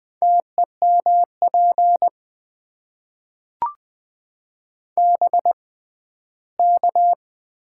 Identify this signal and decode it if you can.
Morse code, heard as a keyed tone, about 20 wpm, spaced as for 14 wpm, 700 Hz. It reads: TEMP BK
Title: Top 90 QSO Elements - 6x Character Spacing - 20wpm